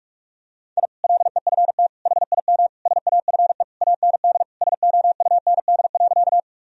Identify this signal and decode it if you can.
Morse code, heard as a keyed tone, about 46 wpm, 700 Hz. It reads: I LEFT HIM SAFE AND SOUND.
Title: Sentences from Top 1000 Words - Review Part 1 - 45wpm